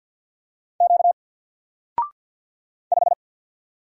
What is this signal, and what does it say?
Morse code, heard as a keyed tone, about 50 wpm, 700 Hz. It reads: Q 5
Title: Single Letter-Number 50wpm